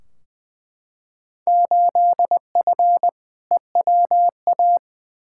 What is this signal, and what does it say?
Morse code, heard as a keyed tone, about 21 wpm, 700 Hz. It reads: 8F EWA
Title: Field Day Exchanges Encore - 20wpm